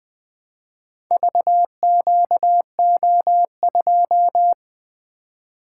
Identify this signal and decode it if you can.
Morse code, heard as a keyed tone, about 20 wpm, 700 Hz. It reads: VQO2
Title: Four Letter-Number Combinations - 20wpm